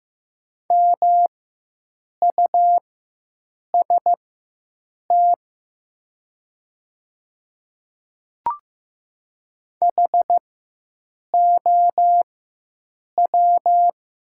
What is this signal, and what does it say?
Morse code, heard as a keyed tone, about 15 wpm, spaced as for 7 wpm, 690 Hz. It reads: MUST HOW
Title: 4x Character Spacing - Top 200 Words - 15wpm